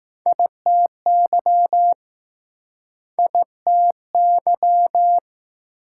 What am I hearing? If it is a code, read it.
Morse code, heard as a keyed tone, about 21 wpm, 700 Hz. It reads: ITY ITY